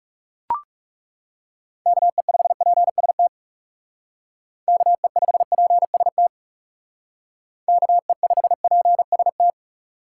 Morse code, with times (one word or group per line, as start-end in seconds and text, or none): 1.9-3.3 KE5PST
4.7-6.3 KE5PST
7.7-9.5 KE5PST